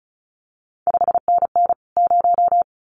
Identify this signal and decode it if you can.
Morse code, heard as a keyed tone, about 36 wpm, 700 Hz. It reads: 5NN 0
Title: CQ WW DX Contest Exchange Encore - 35wpm